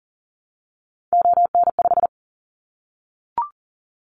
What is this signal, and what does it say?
Morse code, heard as a keyed tone, about 40 wpm, 700 Hz. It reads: ON5
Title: Callsign Prefixes for Top 50 DXCC - 40wpm